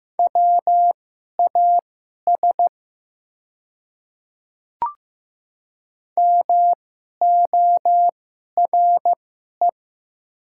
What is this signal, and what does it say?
Morse code, heard as a keyed tone, about 15 wpm, spaced as for 11 wpm, 700 Hz. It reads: WAS MORE
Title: 2x Character Spacing - Top 100 Words - 15wpm